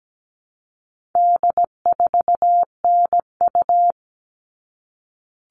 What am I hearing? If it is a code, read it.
Morse code, heard as a keyed tone, about 17 wpm, 700 Hz. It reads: D4NU
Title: All Cut Numbers - 4-Digit 17wpm